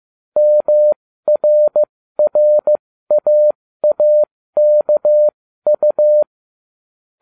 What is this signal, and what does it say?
Morse code, heard as a keyed tone, about 15 wpm, spaced as for 13 wpm, 600 Hz. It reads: MRRAAKU